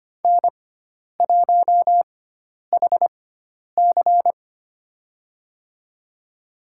Morse code, heard as a keyed tone, about 25 wpm, spaced as for 10 wpm, 700 Hz. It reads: N1HC
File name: Lesson 16g - Letter c - Call Signs 25wpm